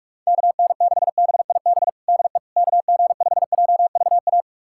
Morse code, heard as a keyed tone, about 46 wpm, 700 Hz. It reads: KN6BIB DE KG5JVA
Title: USA Callsign DE Callsign - 6 Characters - 45wpm